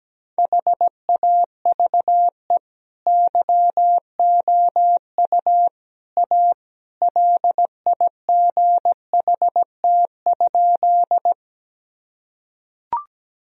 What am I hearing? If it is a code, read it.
Morse code, heard as a keyed tone, about 17 wpm, 700 Hz. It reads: HAVE YOU A LIGHT?
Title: Sentences from Top 1000 Words - Review Part 1 - 17wpm